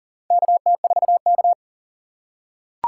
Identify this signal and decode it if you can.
Morse code, heard as a keyed tone, about 40 wpm, 700 Hz. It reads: KT4K
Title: USA Call Signs - 4 Characters - 2x Character Spacing - 40wpm